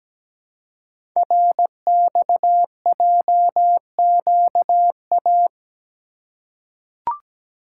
Morse code, heard as a keyed tone, about 17 wpm, 700 Hz. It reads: RXJQA
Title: Code Groups - 5 Letters - 17wpm